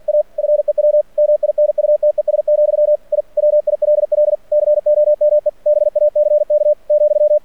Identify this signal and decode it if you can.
Morse code, heard as a keyed tone, about 49 wpm, 600 Hz. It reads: A FEW MINUTES, I WILL COME BACK ,
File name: archived / CRSnet